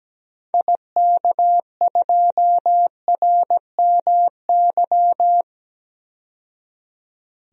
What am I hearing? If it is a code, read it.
Morse code, heard as a keyed tone, about 17 wpm, 700 Hz. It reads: IK2RMY
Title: DXCC-100 Call Signs - 6 Characters - 17wpm